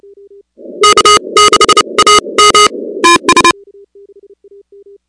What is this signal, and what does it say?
Morse code, heard as a keyed tone, about 30 wpm, spaced as for 25 wpm, 1250 Hz, with noise. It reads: K6AM
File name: contests / 2019-cw-wpx-kt5j-k5pi- -2330 0-16